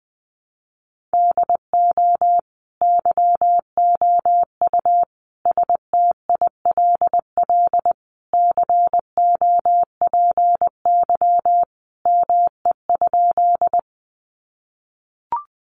Morse code, heard as a keyed tone, about 19 wpm, 700 Hz. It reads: DO YOU STILL COPY ME?
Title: Sentences from Top 1400 Words 20wpm